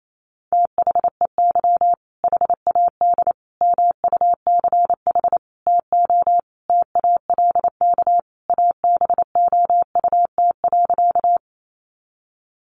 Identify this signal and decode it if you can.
Morse code, heard as a keyed tone, about 28 wpm, 700 Hz. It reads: THEY HAD MUCH TO TALK ABOUT.
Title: Sentences from Top 400 Words 28wpm